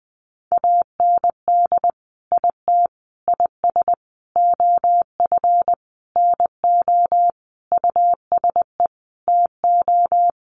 Morse code, heard as a keyed tone, about 20 wpm, 700 Hz. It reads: AND IT IS OF NO USE TO
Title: Sentences from Top 1100 Words 20wpm